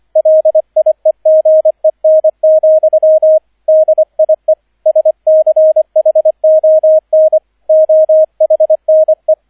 Morse code, heard as a keyed tone, about 25 wpm, 630 Hz, with radio noise. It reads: LIEGEN, DIE SCHON OHNE